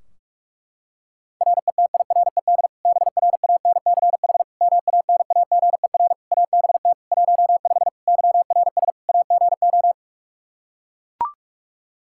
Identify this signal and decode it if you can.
Morse code, heard as a keyed tone, about 42 wpm, 700 Hz. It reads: RETIRED BRANCH MANAGER ABT 15 YRS AGO
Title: Practice QSO - Exchange 3 - More Info - 45wpm